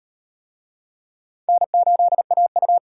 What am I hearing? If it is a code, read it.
Morse code, heard as a keyed tone, about 35 wpm, 700 Hz. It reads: N8AU